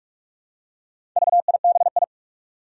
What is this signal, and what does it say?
Morse code, heard as a keyed tone, about 45 wpm, 700 Hz. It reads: VIBI
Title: Four Challenging Letter-Number Combinations 45wpm